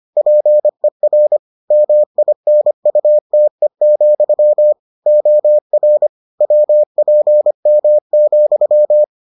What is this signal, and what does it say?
Morse code, heard as a keyed tone, about 25 wpm, 600 Hz. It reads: PER MINUTE, OR WPM,